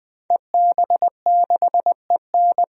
Morse code, heard as a keyed tone, about 20 wpm, 700 Hz. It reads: EB6EN
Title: DXCC-100 Call Signs - 5 Characters - Repeated 2x - 20wpm